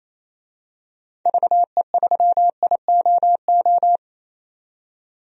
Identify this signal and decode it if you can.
Morse code, heard as a keyed tone, about 27 wpm, 700 Hz. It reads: VE3IOO
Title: DXCC-100 Call Signs - 6 Characters - 28wpm